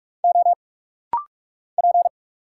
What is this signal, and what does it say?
Morse code, heard as a keyed tone, about 46 wpm, 700 Hz. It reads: O P
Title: Single Letter-Number Alt - Rapid-Fire 45wpm